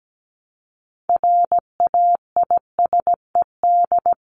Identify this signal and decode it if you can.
Morse code, heard as a keyed tone, about 17 wpm, 700 Hz. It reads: RAISED
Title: ED Suffix - Top 100 Words with Base Words 17wpm